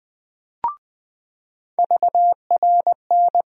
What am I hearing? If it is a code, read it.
Morse code, heard as a keyed tone, about 21 wpm, 700 Hz. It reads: VRN
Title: USA Call Sign Postfixes - 3 Letters 20wpm